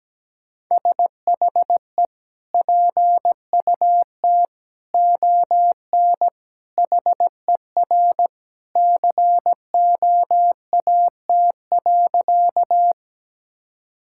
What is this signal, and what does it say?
Morse code, heard as a keyed tone, about 17 wpm, 700 Hz. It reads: SHE PUT ON HER COAT.
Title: Sentences from Top 1600 Words 17wpm